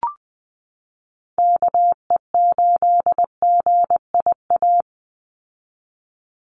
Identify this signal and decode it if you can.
Morse code, heard as a keyed tone, about 20 wpm, 700 Hz. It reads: KE8GIA
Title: USA Call Signs 20wpm